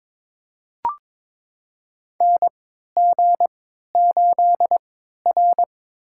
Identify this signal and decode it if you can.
Morse code, heard as a keyed tone, about 21 wpm, spaced as for 12 wpm, 700 Hz. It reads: NG8R
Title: USA Call Signs - 3x Character Spacing - 22wpm